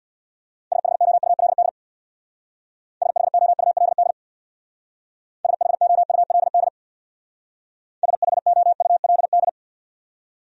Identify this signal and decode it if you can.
Morse code, heard as a keyed tone, about 64 wpm, 700 Hz. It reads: SHOULD SHOULD SHOULD SHOULD